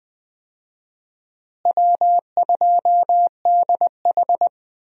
Morse code, heard as a keyed tone, about 20 wpm, 700 Hz. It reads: W2DH
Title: USA Call Signs - 4 Characters - Repeated 3x - 20wpm